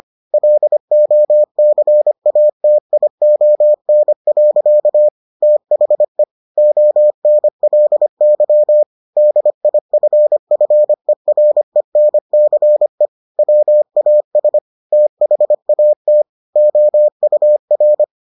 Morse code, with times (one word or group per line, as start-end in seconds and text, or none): 0.3-5.1 LOCATION.
5.4-6.2 THE
6.6-8.8 ONLY
9.2-13.1 DIFFERENCE
13.4-14.6 WAS
14.9-16.2 THAT
16.6-18.0 OUR